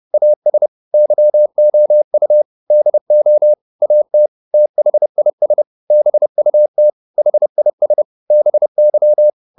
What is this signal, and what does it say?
Morse code, heard as a keyed tone, about 30 wpm, 600 Hz. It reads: AS YOU DO AT THIS BUT HIS BY
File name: Top-100-Words-30